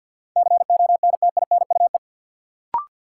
Morse code, heard as a keyed tone, about 52 wpm, 700 Hz. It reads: CONTINUE
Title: Sentences from Top 1100 Words - New Word Drill Alt - 50wpm